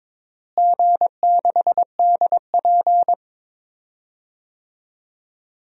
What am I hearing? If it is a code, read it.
Morse code, heard as a keyed tone, about 22 wpm, 700 Hz. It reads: G6DP
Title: DXCC-100 Call Signs - 4 Characters - Repeated 3x - 22wpm